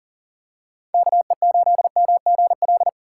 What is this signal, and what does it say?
Morse code, heard as a keyed tone, about 38 wpm, 700 Hz. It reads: KE8MGL
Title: USA Call Signs - 6 Characters - 4x Character Spacing - 40wpm